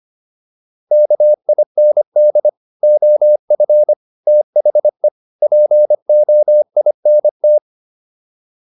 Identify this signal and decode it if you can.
Morse code, heard as a keyed tone, about 25 wpm, 600 Hz. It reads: KIND OF THE POINT